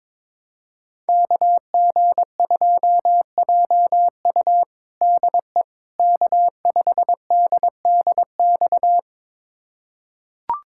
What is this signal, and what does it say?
Morse code, heard as a keyed tone, about 22 wpm, 700 Hz. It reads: KG2JU DE K5DDX
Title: USA Callsign DE Callsign - 5 Characters - 22wpm